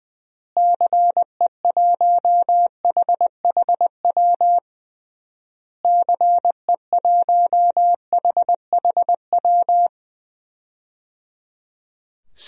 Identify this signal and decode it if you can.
Morse code, heard as a keyed tone, about 20 wpm, 700 Hz. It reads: CE1HHW CE1HHW
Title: DXCC-100 Call Signs - 6 Characters - Repeated 3x - 20wpm